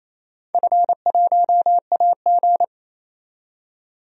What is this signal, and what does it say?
Morse code, heard as a keyed tone, about 28 wpm, 700 Hz. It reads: F1AG
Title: DXCC-100 Call Signs - 4 Characters - 28wpm